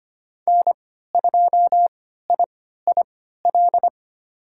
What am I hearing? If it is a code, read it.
Morse code, heard as a keyed tone, about 25 wpm, spaced as for 14 wpm, 700 Hz. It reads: N2IIL
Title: USA Call Signs - 3x Character Spacing - 25wpm